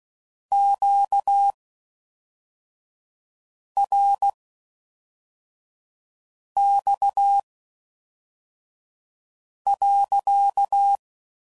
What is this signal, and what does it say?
Morse code, heard as a keyed tone, about 16 wpm, spaced as for 4 wpm, 800 Hz. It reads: QRX.